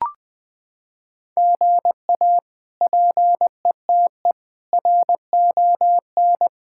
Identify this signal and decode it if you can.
Morse code, heard as a keyed tone, about 20 wpm, 700 Hz. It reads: GA PETE RON
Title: K1USN SST QSO Practice - 20wpm